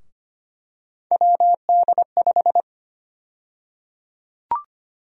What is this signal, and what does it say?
Morse code, heard as a keyed tone, about 25 wpm, 700 Hz. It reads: WD5
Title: USA Call Sign Prefixes - 25wpm